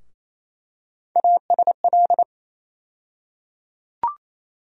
Morse code, heard as a keyed tone, about 28 wpm, 700 Hz. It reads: ASL
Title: USA Call Sign Postfixes - 3 Letters 28wpm